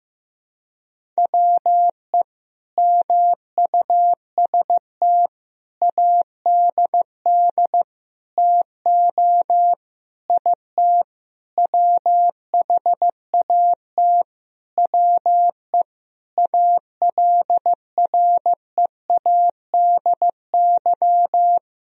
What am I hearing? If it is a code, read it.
Morse code, heard as a keyed tone, about 16 wpm, 700 Hz. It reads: WE MUST ADD TO IT WHAT WE ALREADY
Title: Sentences from Top 1000 Words - Review Part 1 - 15wpm